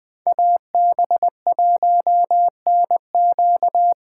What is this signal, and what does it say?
Morse code, heard as a keyed tone, about 20 wpm, 700 Hz. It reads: AB1NQ